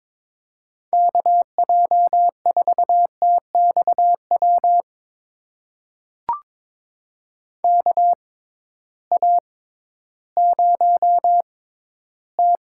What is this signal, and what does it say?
Morse code, heard as a keyed tone, about 22 wpm, spaced as for 13 wpm, 700 Hz. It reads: KJ4TXW KA0T
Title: USA Call Signs - 6x Character Spacing - 22wpm